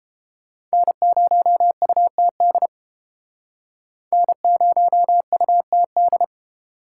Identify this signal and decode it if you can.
Morse code, heard as a keyed tone, about 32 wpm, 700 Hz. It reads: N0UTD N0UTD